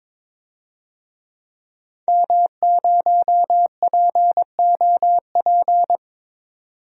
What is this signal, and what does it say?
Morse code, heard as a keyed tone, about 22 wpm, 700 Hz. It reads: M0POP
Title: DXCC-100 Call Signs - 5 Characters - 22wpm